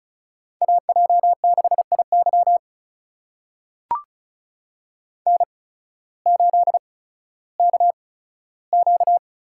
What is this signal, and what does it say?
Morse code, heard as a keyed tone, about 36 wpm, spaced as for 16 wpm, 700 Hz. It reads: AJ6IY N8KQ
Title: USA Call Signs - 5 Characters - 8x Character Spacing - 35wpm